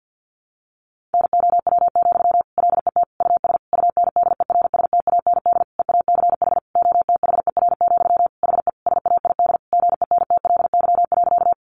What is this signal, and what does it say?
Morse code, heard as a keyed tone, about 50 wpm, 700 Hz. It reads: NOW, LET US UNDERSTAND EACH OTHER, HE SAID GENTLY.
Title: Sentences from Top 1500 Words 50wpm